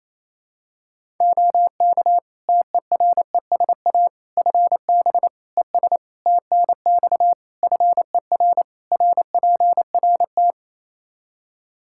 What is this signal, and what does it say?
Morse code, heard as a keyed tone, about 28 wpm, 700 Hz. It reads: OK TERESA FB ES TNX FER RPRT
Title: Practice QSO - Exchange 2 - Info - 28wpm